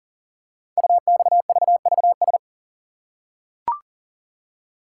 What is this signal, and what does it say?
Morse code, heard as a keyed tone, about 38 wpm, 700 Hz. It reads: UXVVS
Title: Code Groups - 5 Letters - 40wpm